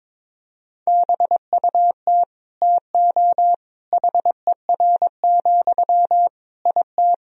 Morse code, heard as a keyed tone, about 22 wpm, 700 Hz. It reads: BUT TO HER, IT
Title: Sentences from Top 1000 Words - Review Part 2 - 22wpm